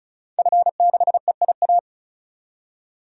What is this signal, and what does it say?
Morse code, heard as a keyed tone, about 36 wpm, 700 Hz. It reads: F6EIA
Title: DXCC-100 Call Signs - 5 Characters - 35wpm